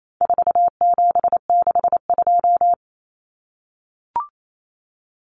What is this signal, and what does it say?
Morse code, heard as a keyed tone, about 28 wpm, 700 Hz. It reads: 4762
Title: Numbers - 4-Digit 28wpm